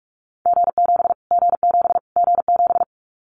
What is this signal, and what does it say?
Morse code, heard as a keyed tone, about 44 wpm, 700 Hz. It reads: G7 G7 G7
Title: Callsign Prefixes for Top 25 DXCC - Repeated 3x - 45wpm